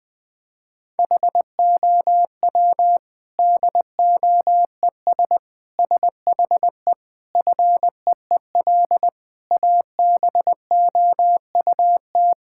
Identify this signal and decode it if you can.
Morse code, heard as a keyed tone, about 20 wpm, 700 Hz. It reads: HOW DOES SHE FEEL ABOUT
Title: Sentences from Top 1000 Words - Review Part 6 - 20wpm